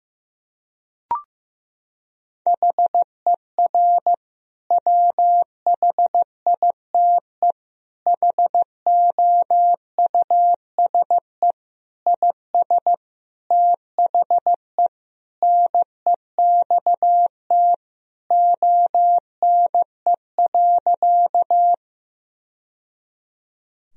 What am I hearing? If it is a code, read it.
Morse code, heard as a keyed tone, about 15 wpm, 700 Hz. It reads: HER WHITE HOUSE IS THE NEXT ONE.
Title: Sentences from Top 1000 Words - Review Part 1 - 15wpm